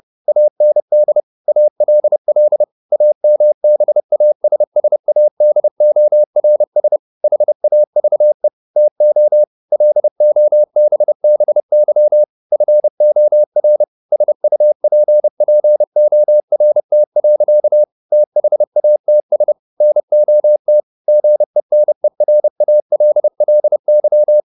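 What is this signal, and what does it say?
Morse code, heard as a keyed tone, about 30 wpm, 600 Hz. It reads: AND ALL AMBASSADORS HAVE TO LOBBY FOR SUPPORT. THATS NOT GENERALLY
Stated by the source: Morse code at 30 wpm, the novel Never by Ken Follett in English, from CW med Gnister.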